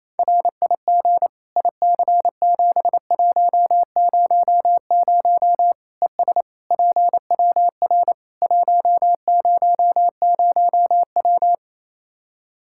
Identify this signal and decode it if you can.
Morse code, heard as a keyed tone, about 27 wpm, 700 Hz. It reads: RIG IC7100 ES PWR 100W
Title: K1USN SST QSO Practice 28wpm